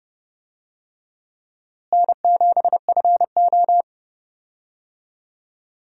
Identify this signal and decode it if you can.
Morse code, heard as a keyed tone, about 30 wpm, 700 Hz. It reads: N7FO
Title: USA Call Signs - 4 Characters - Repeated 3x - 30wpm